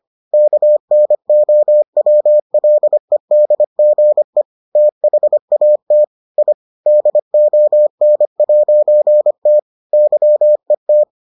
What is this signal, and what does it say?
Morse code, heard as a keyed tone, about 25 wpm, 600 Hz. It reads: KNOWLEDGE THAT I DON'T YET